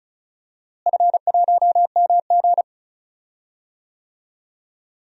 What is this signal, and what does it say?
Morse code, heard as a keyed tone, about 34 wpm, 700 Hz. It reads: F1MG
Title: DXCC-100 Call Signs - 4 Characters - Repeated 3x - 35wpm